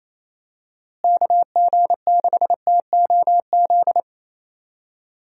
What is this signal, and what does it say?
Morse code, heard as a keyed tone, about 28 wpm, 700 Hz. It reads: KG6TOZ
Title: USA Call Signs - 6 Characters - Repeated 2x - 28wpm